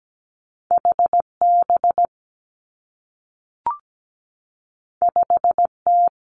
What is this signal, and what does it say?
Morse code, heard as a keyed tone, about 17 wpm, 700 Hz. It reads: H B 5 T
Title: Two Challenging Letter-Number Combinations 17wpm